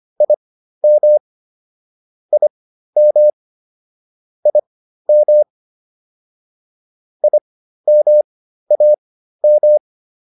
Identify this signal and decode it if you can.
Morse code, heard as a keyed tone, about 25 wpm, spaced as for 13 wpm, 600 Hz. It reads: IM IM IM IMAM